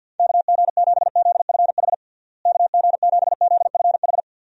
Morse code, heard as a keyed tone, about 50 wpm, 700 Hz. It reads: KG7ZFH KG7ZFH
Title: USA Call Signs - 6 Characters - Repeated 2x - 50wpm